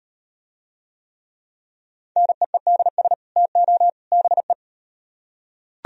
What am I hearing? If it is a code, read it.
Morse code, heard as a keyed tone, about 35 wpm, 700 Hz. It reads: NEEDS TO BE